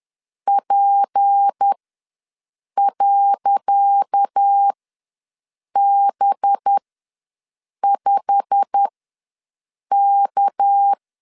Morse code, heard as a keyed tone, about 10 wpm, spaced as for 6 wpm, 800 Hz. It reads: P.B5K
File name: LEZIONI / 10 SIGNS / 10C DOT ws 10wpm 4213